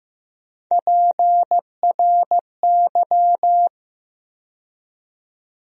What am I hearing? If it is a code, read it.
Morse code, heard as a keyed tone, about 16 wpm, 700 Hz. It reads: PRY
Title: USA Call Sign Postfixes - 3 Letters 15wpm